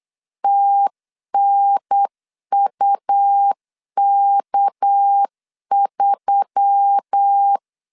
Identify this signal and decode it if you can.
Morse code, heard as a keyed tone, about 8 wpm, 800 Hz. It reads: TNUK3
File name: CW / DET / 1 6 1500C S C 8wpm 5219